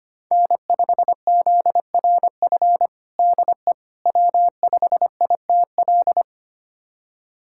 Morse code, heard as a keyed tone, about 25 wpm, 700 Hz. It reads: N5ZRF DE W5ITL
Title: USA Callsign DE Callsign - 5 Characters - 25wpm